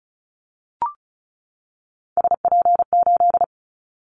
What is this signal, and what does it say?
Morse code, heard as a keyed tone, about 36 wpm, 700 Hz. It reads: SP8
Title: Callsign Prefixes for Top 25 DXCC - 35wpm